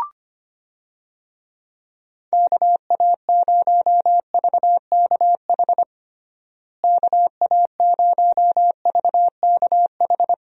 Morse code, heard as a keyed tone, about 25 wpm, 700 Hz. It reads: KA0VKH KA0VKH
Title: USA Call Signs - 6 Characters - Repeated 3x - 25wpm